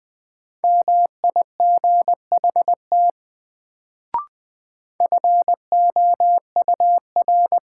Morse code, heard as a keyed tone, about 20 wpm, 700 Hz. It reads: MIGHT FOUR
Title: Sentences from Top 300 Words - New Word Drill - 20wpm